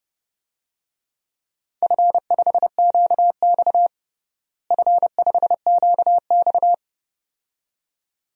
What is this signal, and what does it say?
Morse code, heard as a keyed tone, about 32 wpm, 700 Hz. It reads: F5QX F5QX
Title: DXCC-100 Call Signs - 4 Characters - Repeated 2x - 30wpm